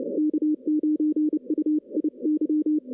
Morse code, heard as a keyed tone, about 30 wpm, 300 Hz. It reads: K9UIY
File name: 2016 arrl dx cw / 2016-ARRL-DX-CW-K5TR- - 0-34